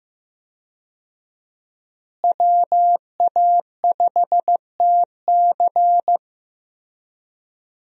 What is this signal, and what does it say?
Morse code, heard as a keyed tone, about 16 wpm, 700 Hz. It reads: WA5TC
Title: USA Call Signs - 5 Characters - Repeated 3x - 15wpm